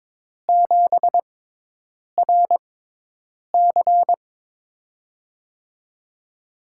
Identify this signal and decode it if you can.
Morse code, heard as a keyed tone, about 22 wpm, spaced as for 8 wpm, 700 Hz. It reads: 7RC
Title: USA Call Signs - 5 Characters - 6x Character Spacing - 22wpm